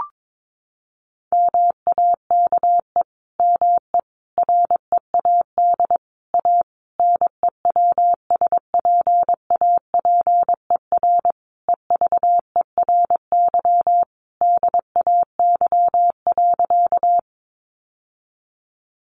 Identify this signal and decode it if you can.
Morse code, heard as a keyed tone, about 22 wpm, 700 Hz. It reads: MAKE ME READ A NEWSPAPER EVERY DAY.
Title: Sentences from Top 1100 Words 22wpm